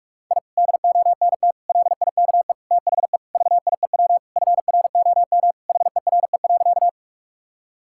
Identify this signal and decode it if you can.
Morse code, heard as a keyed tone, about 46 wpm, 700 Hz. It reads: I DONT LIKE THE VIEW FROM HERE.
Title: Sentences from Top 1000 Words - Review Part 6 - 45wpm